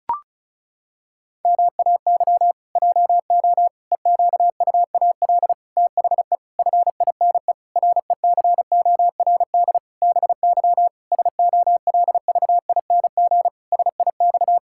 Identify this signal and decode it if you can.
Morse code, heard as a keyed tone, about 34 wpm, 700 Hz. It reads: MAY JO EQUAL THE FINE RECORD BY SOLVING SIX
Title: Pangram Challenge v2 - 35wpm